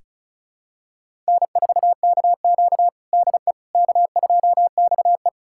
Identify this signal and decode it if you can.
Morse code, heard as a keyed tone, about 36 wpm, 700 Hz. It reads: N4KQ DE K2XE
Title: USA Callsign DE Callsign - 4 Characters - 35wpm